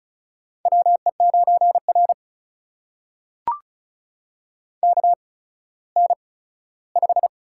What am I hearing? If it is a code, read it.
Morse code, heard as a keyed tone, about 34 wpm, spaced as for 17 wpm, 700 Hz. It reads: WE9R KN5
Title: USA Call Signs - 4 Characters - 8x Character Spacing - 35wpm